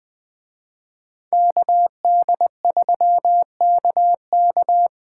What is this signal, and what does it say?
Morse code, heard as a keyed tone, about 19 wpm, 700 Hz. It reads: KD3KK